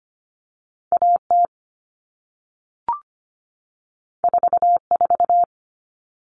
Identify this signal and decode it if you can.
Morse code, heard as a keyed tone, about 25 wpm, 700 Hz. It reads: A T 4 4
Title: All Cut Numbers - 2-Digit 25wpm